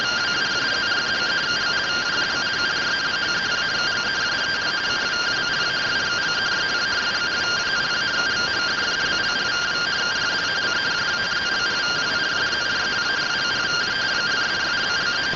Two-tone teleprinter signal, about 170 Hz shift.